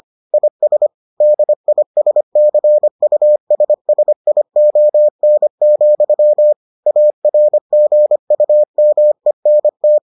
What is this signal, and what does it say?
Morse code, heard as a keyed tone, about 25 wpm, 600 Hz. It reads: IS DISCUSSION, ARGUMENT